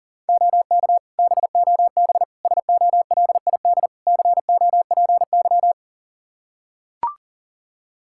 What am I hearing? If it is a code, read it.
Morse code, heard as a keyed tone, about 37 wpm, 700 Hz. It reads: OK BOB SOLID COPY
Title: Practice QSO - Exchange 3 - More Info - 40wpm